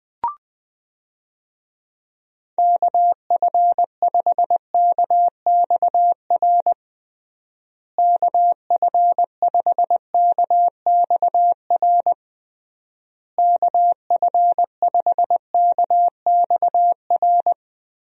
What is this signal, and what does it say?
Morse code, heard as a keyed tone, about 21 wpm, 700 Hz. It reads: KF5KXR KF5KXR KF5KXR
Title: USA Call Signs - 6 Characters - Repeated 3x - 20wpm